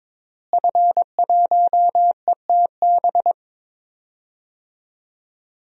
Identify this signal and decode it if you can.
Morse code, heard as a keyed tone, about 22 wpm, 700 Hz. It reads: F1ETB
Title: DXCC-100 Call Signs - 5 Characters - Repeated 3x - 22wpm